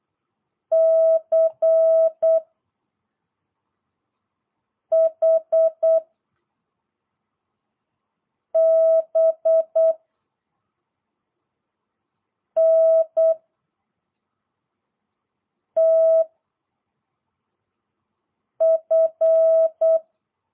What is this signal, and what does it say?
Morse code, heard as a keyed tone, about 8 wpm, 660 Hz. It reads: C H B N T F